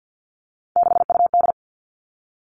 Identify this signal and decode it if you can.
Morse code, heard as a keyed tone, about 50 wpm, 700 Hz. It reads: 6UD
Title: Three Challenging Letter-Number Combinations 50wpm